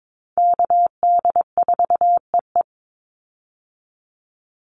Morse code, heard as a keyed tone, about 22 wpm, 700 Hz. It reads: KD4EE
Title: Call Signs Repeated 3x - 22wpm